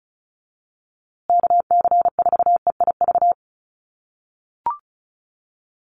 Morse code, heard as a keyed tone, about 34 wpm, 700 Hz. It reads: KC4EIV
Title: USA Call Signs Repeated 3x 35wpm